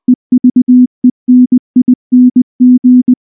Morse code, heard as a keyed tone, about 19 wpm, 250 Hz. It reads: EVENING
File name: morse-training-20WPM